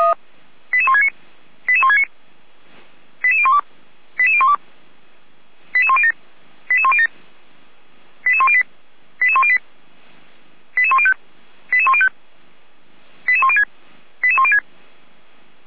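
Five-tone selective call (ZVEI), heard as quick runs of five tones, tones of about 70 ms. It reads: 88169 88169 88012 88012 88107 88107 88108 88108 88105 88105 88106 88106
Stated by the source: ZVEI five-tone selective call received by SDR.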